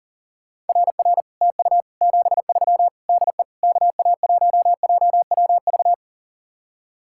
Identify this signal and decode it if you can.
Morse code, heard as a keyed tone, about 51 wpm, 700 Hz. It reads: RR TU 73 DE KA1JWV